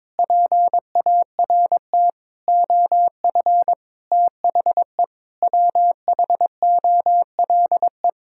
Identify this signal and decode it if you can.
Morse code, heard as a keyed tone, about 22 wpm, 700 Hz. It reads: PART OF THE WHOLE
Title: Sentences from Top 1000 Words - Review Part 6 - 22wpm